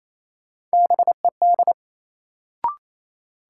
Morse code, heard as a keyed tone, about 28 wpm, 700 Hz. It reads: BED
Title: Sentences from Top 500 Words - New Word Drill - 28wpm